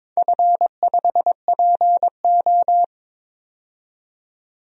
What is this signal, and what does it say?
Morse code, heard as a keyed tone, about 21 wpm, 700 Hz. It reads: F5PO
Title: DXCC-100 Call Signs - 4 Characters - 22wpm